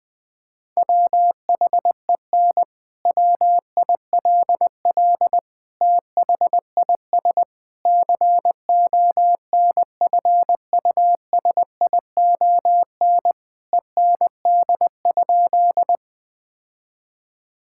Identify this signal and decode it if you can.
Morse code, heard as a keyed tone, about 21 wpm, 700 Hz. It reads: WHEN WILL THIS CONFUSION END?
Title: Sentences from Top 1600 Words 20wpm